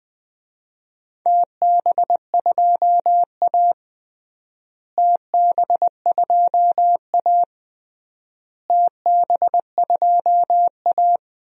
Morse code, heard as a keyed tone, about 21 wpm, 700 Hz. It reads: TB2A TB2A TB2A
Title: DXCC-100 Call Signs - 4 Characters - Repeated 3x - 20wpm